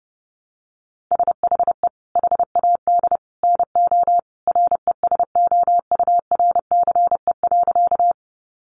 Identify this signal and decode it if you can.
Morse code, heard as a keyed tone, about 30 wpm, 700 Hz. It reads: SHE HAD NO RESOURCE.
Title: Sentences from Top 1300 Words 30wpm